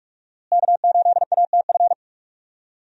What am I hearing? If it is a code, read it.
Morse code, heard as a keyed tone, about 47 wpm, 700 Hz. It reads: K8ATF
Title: USA Call Signs - 5 Characters - Repeated 2x - 45wpm